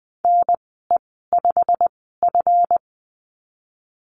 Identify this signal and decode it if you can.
Morse code, heard as a keyed tone, about 20 wpm, spaced as for 15 wpm, 700 Hz. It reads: NE5F
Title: 179 Lesson 29g - Slash - Call Signs 20wpm